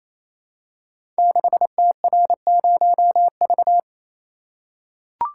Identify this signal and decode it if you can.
Morse code, heard as a keyed tone, about 29 wpm, 700 Hz. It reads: 6TR0V
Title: Code Groups - 5 Letter:Numbers - 28wpm